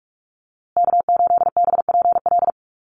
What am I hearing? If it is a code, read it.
Morse code, heard as a keyed tone, about 44 wpm, 700 Hz. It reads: K8BPL
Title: USA Call Signs Repeated 2x 45wpm